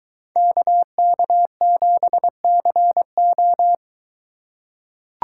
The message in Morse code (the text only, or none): KK7CO